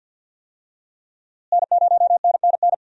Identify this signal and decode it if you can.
Morse code, heard as a keyed tone, about 49 wpm, 700 Hz. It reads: N0NNN
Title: USA Call Signs - 5 Characters - Repeated 3x - 50wpm